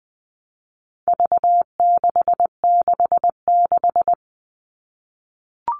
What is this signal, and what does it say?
Morse code, heard as a keyed tone, about 20 wpm, 700 Hz. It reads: V666
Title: All Cut Numbers - 4-Digit 20wpm